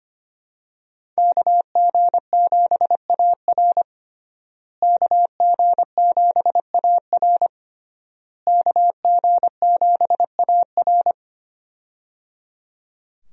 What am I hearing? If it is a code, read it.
Morse code, heard as a keyed tone, about 25 wpm, 700 Hz. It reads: KG7AR KG7AR KG7AR
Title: USA Call Signs - 5 Characters - Repeated 3x - 25wpm